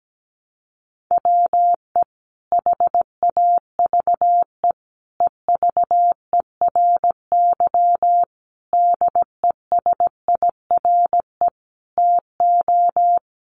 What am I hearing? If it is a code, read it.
Morse code, heard as a keyed tone, about 17 wpm, 700 Hz. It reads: WE HAVE EVERY DESIRE TO
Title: Sentences from Top 1100 Words 17wpm